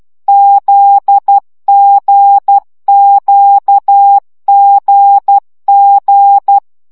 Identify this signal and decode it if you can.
Morse code, heard as a keyed tone, about 12 wpm, 800 Hz. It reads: ZGQGG
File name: CW gyakorlas g z q 10 baud